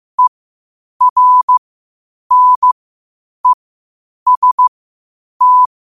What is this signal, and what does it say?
Morse code, heard as a keyed tone, about 15 wpm, spaced as for 8 wpm, 1000 Hz. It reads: ERNEST
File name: qotdfiles / 8wpm